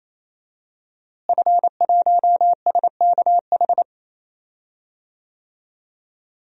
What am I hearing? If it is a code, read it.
Morse code, heard as a keyed tone, about 28 wpm, 700 Hz. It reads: F1SKH
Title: DXCC-100 Call Signs - 5 Characters - Repeated 3x - 28wpm